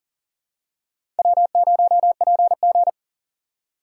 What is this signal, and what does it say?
Morse code, heard as a keyed tone, about 38 wpm, 700 Hz. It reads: W0PG
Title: USA Call Signs - 4 Characters - 4x Character Spacing - 40wpm